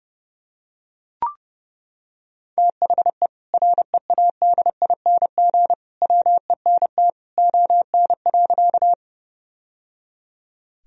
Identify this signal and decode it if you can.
Morse code, heard as a keyed tone, about 30 wpm, 700 Hz. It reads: THE READING WENT ON.
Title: Sentences from Top 1000 Words - Review Part 3 - 30wpm